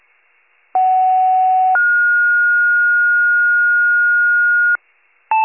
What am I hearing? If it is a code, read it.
Two-tone page, heard as a one-second tone, then a three-second tone, about 750 then 1430 Hz. It is heard from 0.7 to 4.8 s.
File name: wcalert